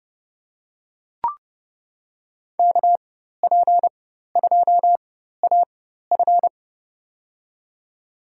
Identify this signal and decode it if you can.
Morse code, heard as a keyed tone, about 30 wpm, spaced as for 14 wpm, 700 Hz. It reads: KP2AF
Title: USA Call Signs - 5 Characters - 4x Character Spacing - 30wpm